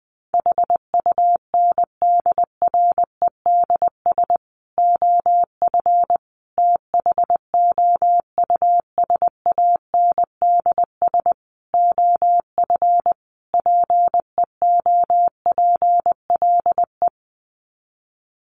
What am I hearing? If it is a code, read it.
Morse code, heard as a keyed tone, about 20 wpm, 700 Hz. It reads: HUNDREDS OF THOUSANDS OF PEOPLE
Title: Sets of 5 Words Encore2 20wpm